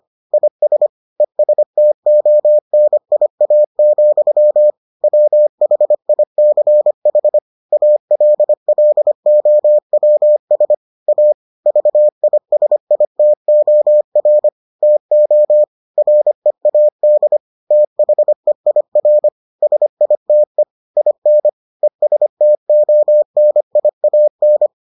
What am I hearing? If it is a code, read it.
Morse code, heard as a keyed tone, about 25 wpm, 600 Hz. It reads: IS ESTONIA, WHICH ALLOWS A VISITOR TO READ THEIR SITE IN ESTONIAN